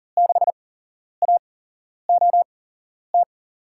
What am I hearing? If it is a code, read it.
Morse code, heard as a keyed tone, about 40 wpm, spaced as for 11 wpm, 700 Hz. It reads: 6AOT
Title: USA Call Signs - 5 Characters - 8x Character Spacing - 40wpm